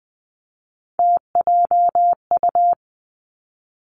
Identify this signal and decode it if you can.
Morse code, heard as a keyed tone, about 20 wpm, 700 Hz. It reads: TJU
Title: Three Challenging Letter-Number Combinations 20wpm